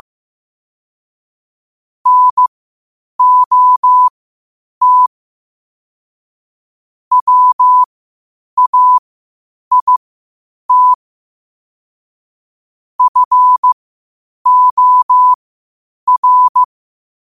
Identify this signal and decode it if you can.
Morse code, heard as a keyed tone, about 15 wpm, spaced as for 8 wpm, 1000 Hz. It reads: NOT WAIT FOR
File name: qotdfiles / 8wpm